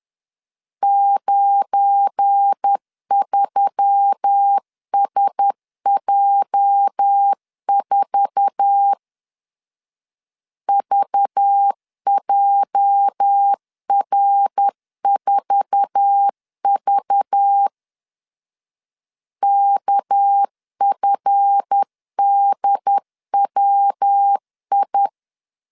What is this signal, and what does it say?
Morse code, heard as a keyed tone, about 10 wpm, 800 Hz. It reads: 93SJ4 VJR4V KFDWI
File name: LEZIONI / 8 YJ45 / 8 YJ4 10wpm 2633